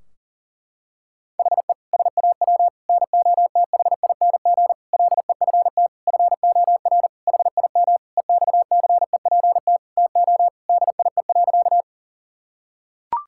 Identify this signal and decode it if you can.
Morse code, heard as a keyed tone, about 40 wpm, 700 Hz. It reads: HE SAW NOTHING LEFT FOR HIM EXCEPT TO DIE.
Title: Sentences from Top 1600 Words 40wpm